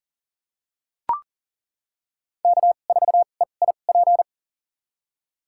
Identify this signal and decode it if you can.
Morse code, heard as a keyed tone, about 40 wpm, spaced as for 29 wpm, 700 Hz. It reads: K4EIP
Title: USA Call Signs - 5 Characters - 2x Character Spacing - 40wpm